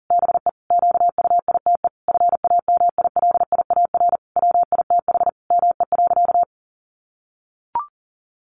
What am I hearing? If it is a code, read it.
Morse code, heard as a keyed tone, about 40 wpm, 700 Hz. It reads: BE QUITE FAMILIAR WITH ME.
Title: Sentences from Top 1000 Words 40wpm